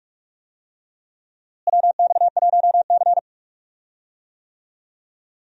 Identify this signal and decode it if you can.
Morse code, heard as a keyed tone, about 46 wpm, 700 Hz. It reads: WX1C
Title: USA Call Signs - 4 Characters - Repeated 2x - 45wpm